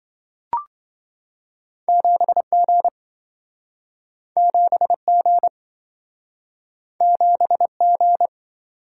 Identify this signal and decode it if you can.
Morse code, heard as a keyed tone, about 27 wpm, 700 Hz. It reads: 7G 7G 7G